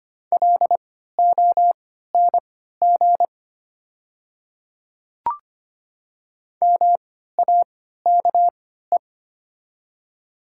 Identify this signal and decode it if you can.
Morse code, heard as a keyed tone, about 25 wpm, spaced as for 14 wpm, 700 Hz. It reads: LONG MAKE
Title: 3x Character Spacing - Top 100 Words - 25wpm